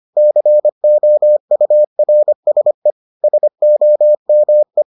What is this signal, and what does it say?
Morse code, heard as a keyed tone, about 25 wpm, 600 Hz. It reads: COURSE SOME